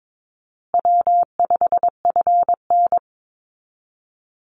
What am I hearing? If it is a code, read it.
Morse code, heard as a keyed tone, about 22 wpm, 700 Hz. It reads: W5FN
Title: Call Signs Repeated 2x - 22wpm